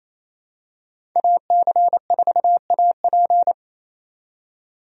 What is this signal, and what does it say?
Morse code, heard as a keyed tone, about 28 wpm, 700 Hz. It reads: AC4AP
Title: USA Call Signs - 5 Characters - 28wpm